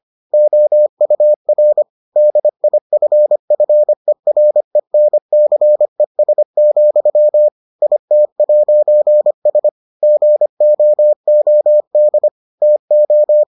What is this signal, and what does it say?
Morse code, heard as a keyed tone, about 25 wpm, 600 Hz. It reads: OUR DIFFERENCES, IT'S GOOD TO